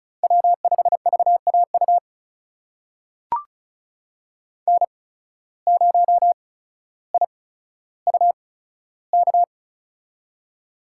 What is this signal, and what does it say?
Morse code, heard as a keyed tone, about 34 wpm, spaced as for 15 wpm, 700 Hz. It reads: W5VAU N0IUK
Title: USA Call Signs - 5 Characters - 8x Character Spacing - 35wpm